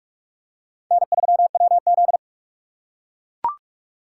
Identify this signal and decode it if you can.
Morse code, heard as a keyed tone, about 41 wpm, 700 Hz. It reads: N3WZ